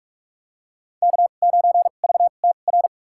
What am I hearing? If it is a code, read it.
Morse code, heard as a keyed tone, about 46 wpm, spaced as for 33 wpm, 700 Hz. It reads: K9VTR
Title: USA Call Signs - 5 Characters - 2x Character Spacing - 45wpm